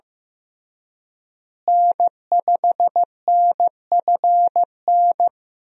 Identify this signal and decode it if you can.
Morse code, heard as a keyed tone, about 15 wpm, 700 Hz. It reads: N5NFN
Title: USA Call Signs - 5 Characters - Repeated 3x - 15wpm